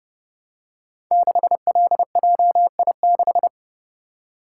Morse code, heard as a keyed tone, about 28 wpm, 700 Hz. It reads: 6LJI6
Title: Code Groups - 5 Letter:Numbers - 30wpm